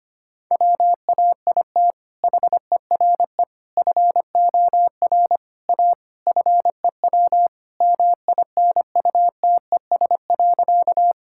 Morse code, heard as a keyed tone, about 25 wpm, 700 Hz. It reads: WAIT HERE FOR A FEW MINUTES.
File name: practice / 25wpm / Sentences with 3-Word Phrases - 25wpm